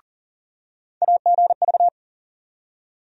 Morse code, heard as a keyed tone, about 38 wpm, 700 Hz. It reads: AGV
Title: USA Call Sign Postfixes - 3 Letters 40wpm